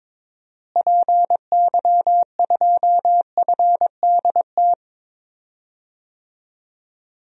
Morse code, heard as a keyed tone, about 22 wpm, 700 Hz. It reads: PY2FDT
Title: DXCC-100 Call Signs - 6 Characters - Repeated 2x - 22wpm